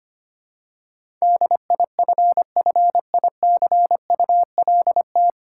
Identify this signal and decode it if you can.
Morse code, heard as a keyed tone, about 25 wpm, 700 Hz. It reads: DIFFICULT
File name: practice / 25wpm / 9 Letter Words - First 3 Letters Given - 25wpm